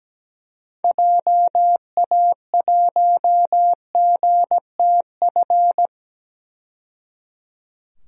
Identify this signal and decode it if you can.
Morse code, heard as a keyed tone, about 17 wpm, 700 Hz. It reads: JA1GTF
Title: DXCC-100 Call Signs - 6 Characters - 17wpm